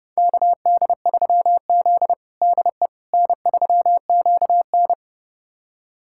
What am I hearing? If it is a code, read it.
Morse code, heard as a keyed tone, about 30 wpm, 700 Hz. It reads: KD3Z DE N3QN
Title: USA Callsign DE Callsign - 4 Characters - 30wpm